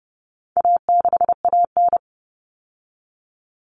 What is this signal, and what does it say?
Morse code, heard as a keyed tone, about 30 wpm, 700 Hz. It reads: A6AN
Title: Common Cut Numbers - 4-Digit 30wpm